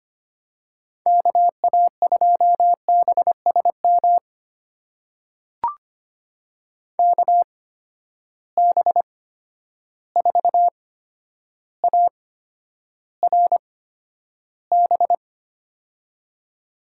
Morse code, heard as a keyed tone, about 25 wpm, spaced as for 11 wpm, 700 Hz. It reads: KA2BSM KB4ARB
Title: USA Call Signs - 6 Characters - 8x Character Spacing - 25wpm